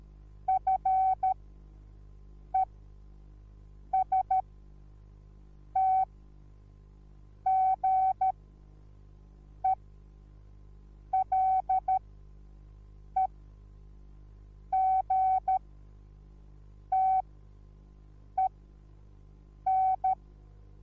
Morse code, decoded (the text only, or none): FESTGELEGTEN